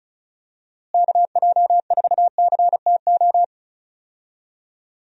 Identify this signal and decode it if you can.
Morse code, heard as a keyed tone, about 34 wpm, 700 Hz. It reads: KJ4CTO
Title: USA Call Signs - 6 Characters - Repeated 2x - 35wpm